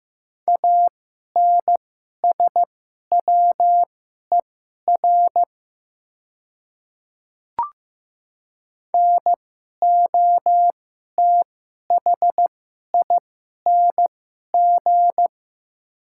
Morse code, heard as a keyed tone, about 15 wpm, spaced as for 11 wpm, 700 Hz. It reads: ANSWER NOTHING
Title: 2x Character Spacing - Top 500 Words - 15wpm